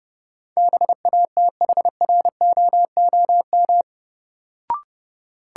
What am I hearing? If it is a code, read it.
Morse code, heard as a keyed tone, about 30 wpm, 700 Hz. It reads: BATHROOM
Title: Sets of 5 Words Encore - New Word Drill - 30wpm